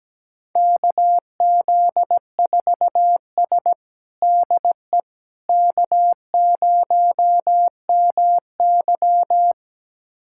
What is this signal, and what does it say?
Morse code, heard as a keyed tone, about 17 wpm, 700 Hz. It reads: KZ4S DE K0MY
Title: USA Callsign DE Callsign - 4 Characters - 17wpm